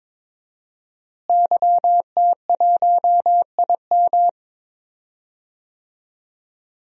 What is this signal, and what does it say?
Morse code, heard as a keyed tone, about 21 wpm, 700 Hz. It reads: YT1IM
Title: DXCC-100 Call Signs - 5 Characters - Repeated 3x - 22wpm